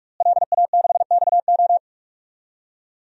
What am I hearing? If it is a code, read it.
Morse code, heard as a keyed tone, about 42 wpm, 700 Hz. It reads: LA6XO
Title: DXCC-100 Call Signs - 5 Characters - Repeated 3x - 45wpm